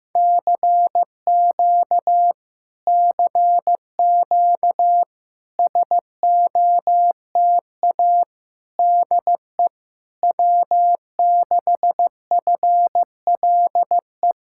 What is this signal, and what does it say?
Morse code, heard as a keyed tone, about 15 wpm, 700 Hz. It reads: CQ CQ SOTA DE W6FLE
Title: Speed-Racing Common QSO Phrases Encore - 15wpm